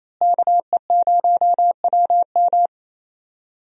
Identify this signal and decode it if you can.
Morse code, heard as a keyed tone, about 28 wpm, 700 Hz. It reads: KE0WM
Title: USA Call Signs - 5 Characters - Repeated 3x - 28wpm